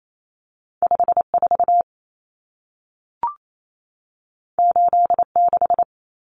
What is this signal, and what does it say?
Morse code, heard as a keyed tone, about 28 wpm, 700 Hz. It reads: 54 86
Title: Numbers - 2-Digit 28wpm